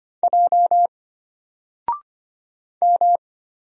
Morse code, heard as a keyed tone, about 25 wpm, 700 Hz. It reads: J M
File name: Single-Letter-Number-25wpm